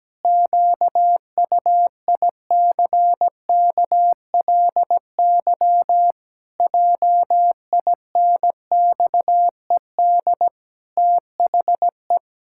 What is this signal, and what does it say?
Morse code, heard as a keyed tone, about 17 wpm, 700 Hz. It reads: QUICKLY JINXED THE
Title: Pangram Challenge v2 - 17wpm